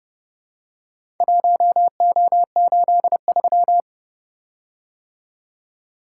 Morse code, heard as a keyed tone, about 30 wpm, 700 Hz. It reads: 1O83